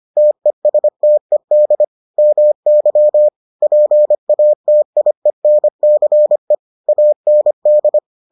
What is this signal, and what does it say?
Morse code, heard as a keyed tone, about 25 wpm, 600 Hz. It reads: TESTED MY PATIENCE AND